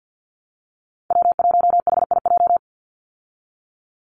Morse code, heard as a keyed tone, about 50 wpm, 700 Hz. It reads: W1HIJ
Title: Call Signs Repeated 3x - 50wpm